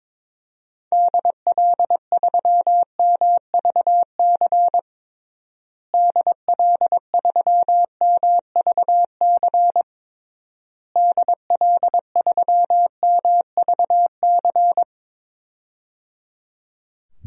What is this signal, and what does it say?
Morse code, heard as a keyed tone, about 22 wpm, 700 Hz. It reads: DL3MVC DL3MVC DL3MVC
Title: DXCC-100 Call Signs - 6 Characters - Repeated 3x - 22wpm